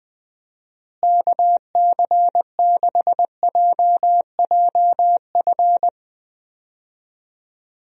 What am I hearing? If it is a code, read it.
Morse code, heard as a keyed tone, about 21 wpm, 700 Hz. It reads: KC6JJF
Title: USA Call Signs - 6 Characters - Repeated 2x - 20wpm